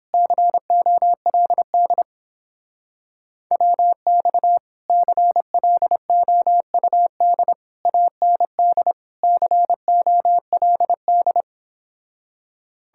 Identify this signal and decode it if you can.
Morse code, heard as a keyed tone, about 31 wpm, 700 Hz. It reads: COLD WX CLOUD AND COLD